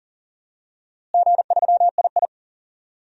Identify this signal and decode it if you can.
Morse code, heard as a keyed tone, about 39 wpm, 700 Hz. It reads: G3II
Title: DXCC-100 Call Signs - 4 Characters - Repeated 3x - 40wpm